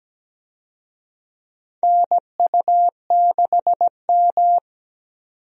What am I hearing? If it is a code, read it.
Morse code, heard as a keyed tone, about 17 wpm, 700 Hz. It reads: NU6M
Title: USA Call Signs - 4 Characters - 17wpm